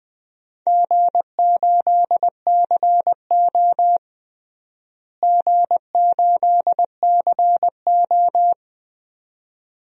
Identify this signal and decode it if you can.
Morse code, heard as a keyed tone, about 20 wpm, 700 Hz. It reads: G8CO G8CO
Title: DXCC-100 Call Signs - 4 Characters - Repeated 2x - 20wpm